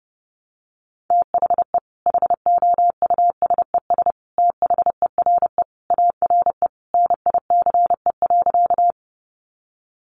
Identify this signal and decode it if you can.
Morse code, heard as a keyed tone, about 30 wpm, 700 Hz. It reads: THE HOUSES THERE ARE NICE.
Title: Easy Phrases 30wpm